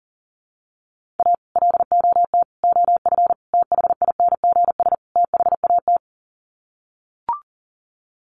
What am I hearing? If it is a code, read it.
Morse code, heard as a keyed tone, about 40 wpm, 700 Hz. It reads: A LOT OF THINGS THAT
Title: Sets of 5 Words 40wpm